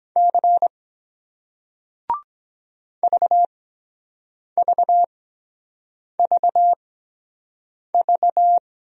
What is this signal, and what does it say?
Morse code, heard as a keyed tone, about 23 wpm, spaced as for 7 wpm, 700 Hz. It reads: C VVVV